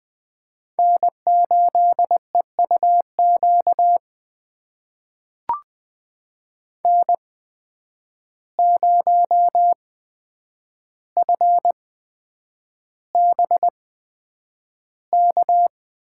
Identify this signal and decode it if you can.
Morse code, heard as a keyed tone, about 20 wpm, spaced as for 9 wpm, 700 Hz. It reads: N8EUQ N0FBK
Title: USA Call Signs - 5 Characters - 8x Character Spacing - 20wpm